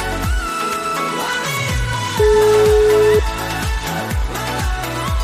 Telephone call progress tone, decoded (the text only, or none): ringback tone